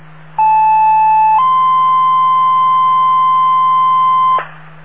Two-tone page, heard as a one-second tone, then a three-second tone, about 870 then 1040 Hz, with noise. It is heard from 0.4 to 4.4 s.